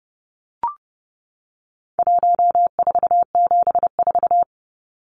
Morse code, heard as a keyed tone, about 29 wpm, 700 Hz. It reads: 1474
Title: Numbers - 4-Digit 30wpm